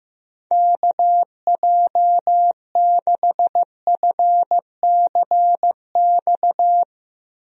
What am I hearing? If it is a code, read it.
Morse code, heard as a keyed tone, about 15 wpm, 700 Hz. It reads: KJ6FCX
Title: USA Call Signs - 6 Characters - Repeated 3x - 15wpm